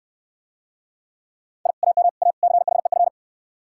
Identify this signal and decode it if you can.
Morse code, heard as a keyed tone, about 67 wpm, 700 Hz. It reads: I AM A CHEF